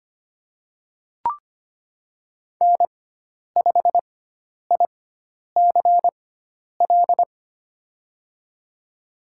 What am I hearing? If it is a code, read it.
Morse code, heard as a keyed tone, about 25 wpm, spaced as for 10 wpm, 700 Hz. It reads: N5ICL
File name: Lesson 16g - Letter c - Call Signs 25wpm